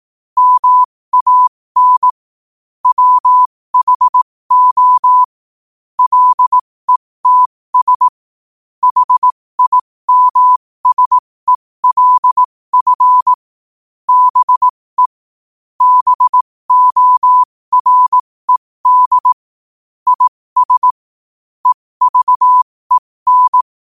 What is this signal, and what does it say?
Morse code, heard as a keyed tone, about 18 wpm, spaced as for 15 wpm, 1000 Hz. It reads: MAN WHO LETS HIMSELF BE BORED IS EVEN
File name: 15wpm